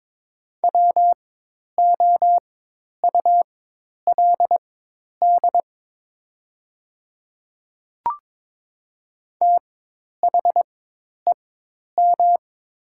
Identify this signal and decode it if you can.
Morse code, heard as a keyed tone, about 22 wpm, spaced as for 10 wpm, 690 Hz. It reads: WOULD THEM
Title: 4x Character Spacing - Top 100 Words - 22wpm